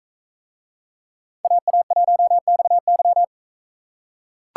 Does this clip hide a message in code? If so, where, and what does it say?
Yes: Morse code from 1.4 to 3.2 s, heard as a keyed tone, about 40 wpm, 700 Hz. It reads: AA1XY